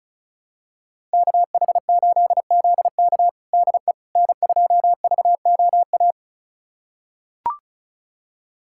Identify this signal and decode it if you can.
Morse code, heard as a keyed tone, about 34 wpm, 700 Hz. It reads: KH8ZK DE N2VOA
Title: USA Callsign DE Callsign - 5 Characters - 35wpm